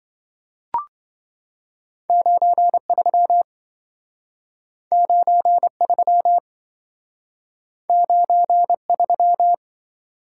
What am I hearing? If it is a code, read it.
Morse code, heard as a keyed tone, about 27 wpm, 700 Hz. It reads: 93 93 93